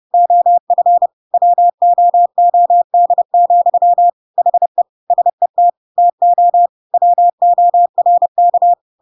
Morse code, heard as a keyed tone, about 30 wpm, 700 Hz. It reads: OF WOOD, HE SET TO WORK
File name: Pinocchio / Pinocchio ch8 30 30 ik1hge